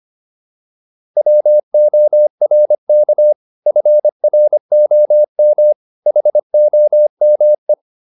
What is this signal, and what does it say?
Morse code, heard as a keyed tone, about 25 wpm, 600 Hz. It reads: WORK FROM HOME